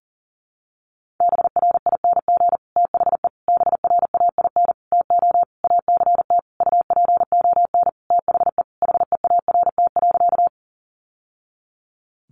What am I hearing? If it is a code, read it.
Morse code, heard as a keyed tone, about 38 wpm, 700 Hz. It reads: BRING THE BRAIN TO ACT UPON THE HEART.
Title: Sentences from Top 1200 Words 40wpm